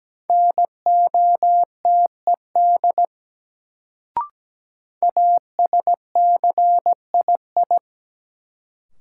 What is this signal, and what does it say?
Morse code, heard as a keyed tone, about 17 wpm, 700 Hz. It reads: NOTED ASCII
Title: Sets of 5 Words - New Word Drill - 17wpm